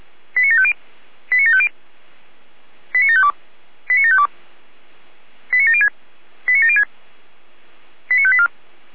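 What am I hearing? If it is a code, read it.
Five-tone selective call (ZVEI), heard as quick runs of five tones, tones of about 70 ms. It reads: 89750 89750 89752 89752 89706 89706 89584
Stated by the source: ZVEI five-tone call received by SDR, from audio.f85360.feuerwehr-orlamuende.org, a fire brigade (Feuerwehr Orlamünde) server.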